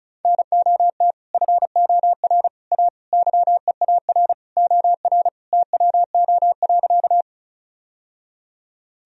Morse code, heard as a keyed tone, about 34 wpm, 700 Hz. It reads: NOT FOR A YEAR OR TWO.
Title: Sentences from Top 1000 Words - Review Part 1 - 35wpm